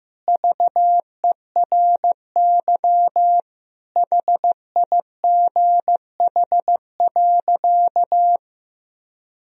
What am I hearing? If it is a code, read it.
Morse code, heard as a keyed tone, about 15 wpm, 700 Hz. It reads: VERY HIGH.
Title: Sentences from Top 1000 Words - Review Part 1 - 15wpm